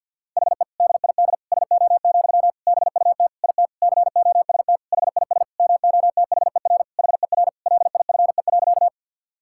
Morse code, heard as a keyed tone, about 52 wpm, 700 Hz. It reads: HE DID SO, BUT IT COST HIS MOTHER HER LIFE.
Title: Sentences from Top 1000 Words - Review Part 5 - 50wpm